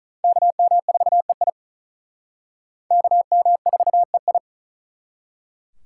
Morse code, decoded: KM4EI KM4EI

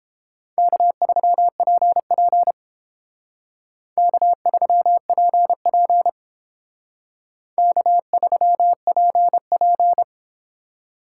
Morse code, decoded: K3PP K3PP K3PP

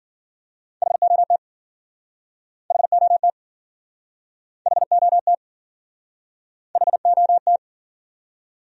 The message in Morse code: HOT HOT HOT HOT

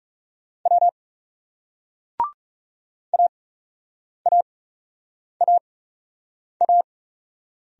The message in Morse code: W AAAA

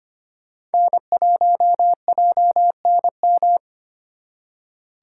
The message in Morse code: N1JNM

